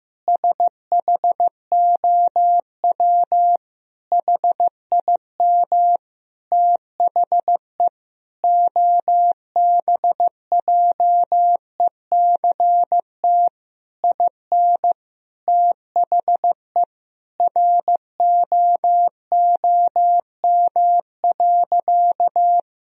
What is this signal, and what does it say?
Morse code, heard as a keyed tone, about 15 wpm, 700 Hz. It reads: SHOW HIM THE OBJECT IN THE ROOM.
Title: Sentences from Top 1000 Words - Review Part 3 - 15wpm